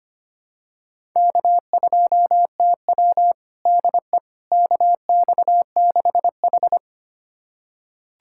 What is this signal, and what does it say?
Morse code, heard as a keyed tone, about 25 wpm, 700 Hz. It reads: K2TW DE KX6H
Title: USA Callsign DE Callsign - 4 Characters - 25wpm